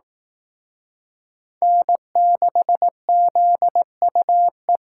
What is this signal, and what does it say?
Morse code, heard as a keyed tone, about 19 wpm, 700 Hz. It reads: N6ZUE
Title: Speed Racing - USA Call Signs 15wpm